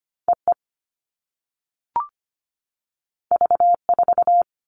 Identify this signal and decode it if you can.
Morse code, heard as a keyed tone, about 25 wpm, 700 Hz. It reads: EE V4
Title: All Cut Numbers - 2-Digit 25wpm